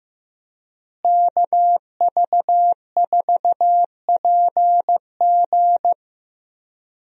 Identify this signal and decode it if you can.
Morse code, heard as a keyed tone, about 15 wpm, 700 Hz. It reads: KV4PG
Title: USA Call Signs - 5 Characters - Repeated 3x - 15wpm